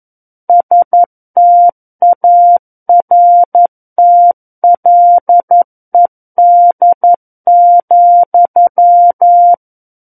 Morse code, heard as a keyed tone, about 11 wpm, 700 Hz. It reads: STARTLED,
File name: 11 WPM